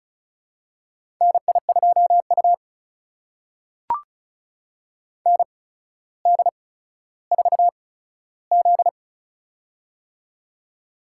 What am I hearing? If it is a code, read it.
Morse code, heard as a keyed tone, about 36 wpm, spaced as for 15 wpm, 700 Hz. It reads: NI2U ND4Z
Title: USA Call Signs - 4 Characters - 8x Character Spacing - 35wpm